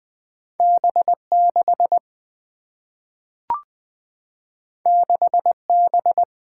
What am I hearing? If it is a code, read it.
Morse code, heard as a keyed tone, about 20 wpm, 700 Hz. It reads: B 6 6 B
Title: Single-Letter Single-Number 20wpm